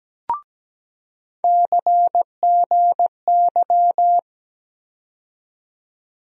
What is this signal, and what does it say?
Morse code, heard as a keyed tone, about 17 wpm, 700 Hz. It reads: CGY
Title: USA Call Sign Postfixes - 3 Letters 17wpm